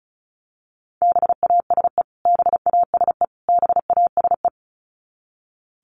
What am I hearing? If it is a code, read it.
Morse code, heard as a keyed tone, about 34 wpm, 700 Hz. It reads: BASE BASE BASE
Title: Top 300 Words Repeated 35wpm